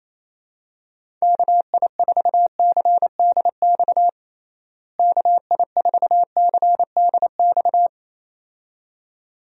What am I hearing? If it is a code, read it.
Morse code, heard as a keyed tone, about 27 wpm, 700 Hz. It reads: KI4CDX KI4CDX
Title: USA Call Signs - 6 Characters - Repeated 2x - 28wpm